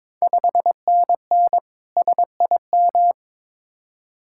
Morse code, heard as a keyed tone, about 22 wpm, 700 Hz. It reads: 5NN SIM
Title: Ontario QSO Party Exchange 22wpm